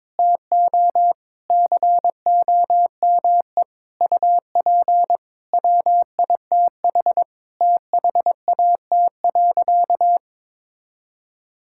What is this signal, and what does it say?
Morse code, heard as a keyed tone, about 22 wpm, 700 Hz. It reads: TO COME UP WITH THAT.
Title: Sentences from Top 1000 Words - Review Part 5 - 22wpm